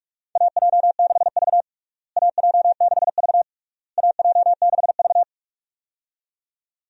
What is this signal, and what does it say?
Morse code, heard as a keyed tone, about 42 wpm, 700 Hz. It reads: AJ6V AJ6V AJ6V
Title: USA Call Signs - 4 Characters - Repeated 3x - 45wpm